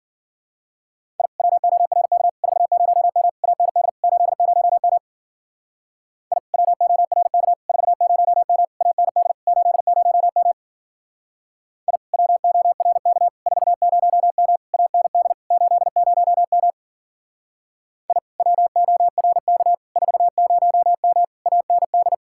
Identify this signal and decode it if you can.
Morse code, heard as a keyed tone, about 49 wpm, 700 Hz. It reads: I WORK 40M AND 80M I WORK 40M AND 80M I WORK 40M AND 80M I WORK 40M AND
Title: Speed-Racing Common QSO Phrases Encore - 40wpm